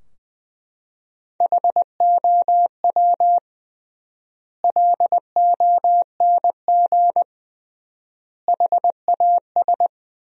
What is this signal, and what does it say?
Morse code, heard as a keyed tone, about 20 wpm, 700 Hz. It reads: HOW LONG HAS
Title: 3x Word-Spacing - Sentences from Top 100 Words - 20wpm